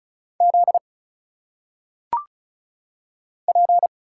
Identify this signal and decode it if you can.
Morse code, heard as a keyed tone, about 36 wpm, 700 Hz. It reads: Z P
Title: Single Letters - Phonetic Pronunciation 35wpm